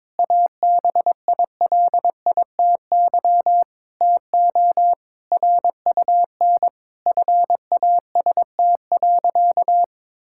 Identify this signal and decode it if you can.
Morse code, heard as a keyed tone, about 22 wpm, 700 Hz. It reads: ABILITY TO RUN FAST.
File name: practice / 22wpm / Sentences with 3-Word Phrases - 22wpm